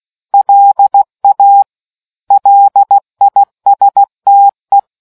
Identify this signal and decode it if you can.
Morse code, heard as a keyed tone, about 16 wpm, 800 Hz. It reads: LA LISTE